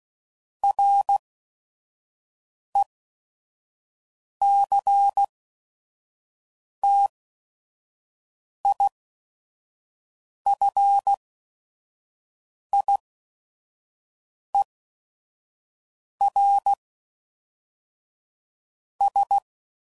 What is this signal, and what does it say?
Morse code, heard as a keyed tone, about 16 wpm, spaced as for 5 wpm, 800 Hz. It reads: RECTIFIERS
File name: CW Code Groups @5WPM